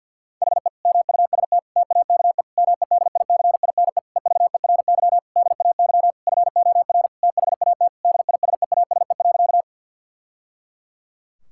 Morse code, heard as a keyed tone, about 52 wpm, 700 Hz. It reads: HE MUST TAKE MEDICINE EVERY DAY FOR THAT DISEASE.
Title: Sentences from Top 1000 Words - Review Part 6 - 50wpm